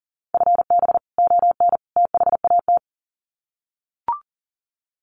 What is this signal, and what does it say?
Morse code, heard as a keyed tone, about 39 wpm, 700 Hz. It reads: FB ON THAT
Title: Common QSO Phrases 40wpm